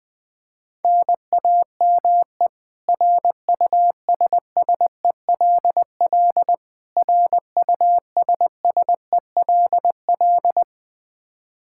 Morse code, heard as a keyed tone, about 20 wpm, 700 Hz. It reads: NAME RUSSELL RUSSELL
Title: K1USN SST QSO Practice 20wpm